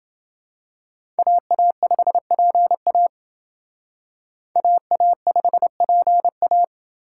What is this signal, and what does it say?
Morse code, heard as a keyed tone, about 29 wpm, 700 Hz. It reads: AA5PA AA5PA